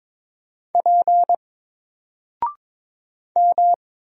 Morse code, heard as a keyed tone, about 22 wpm, 700 Hz. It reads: P M
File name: Single-Letter-Number-22wpm